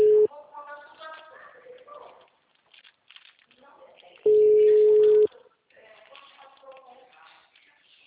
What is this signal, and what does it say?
Ringback tone (Europe, 425 Hz), with speech.